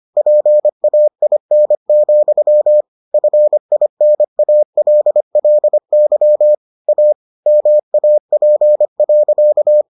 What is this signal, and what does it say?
Morse code, heard as a keyed tone, about 25 wpm, 600 Hz. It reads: PAIN, FINALLY A MAP.